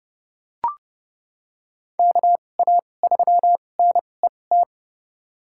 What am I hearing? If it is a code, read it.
Morse code, heard as a keyed tone, about 30 wpm, spaced as for 22 wpm, 700 Hz. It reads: KA3NET
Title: USA Call Signs - 6 Characters - 2x Character Spacing - 30wpm